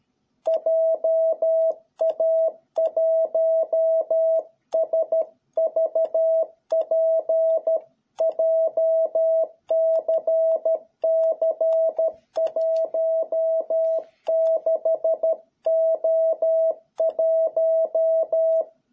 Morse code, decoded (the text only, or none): JA1SVP JCC16O1